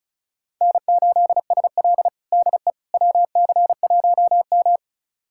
Morse code, heard as a keyed tone, about 36 wpm, 700 Hz. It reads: N8SL DE WC1M
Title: USA Callsign DE Callsign - 4 Characters - 35wpm